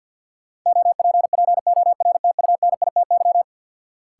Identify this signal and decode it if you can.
Morse code, heard as a keyed tone, about 48 wpm, 700 Hz. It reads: OPPORTUNITY
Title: Sentences from Top 1000 Words - New Word Drill - 50wpm